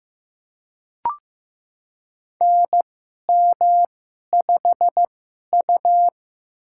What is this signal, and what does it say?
Morse code, heard as a keyed tone, about 16 wpm, spaced as for 11 wpm, 700 Hz. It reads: NM5U
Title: USA Call Signs - 2x Character Spacing - 15wpm